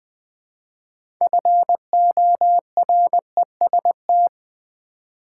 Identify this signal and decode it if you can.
Morse code, heard as a keyed tone, about 20 wpm, 700 Hz. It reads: FOREST
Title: Build A Word - 2 Letters At A Time - 6-Letter Words - 20wpm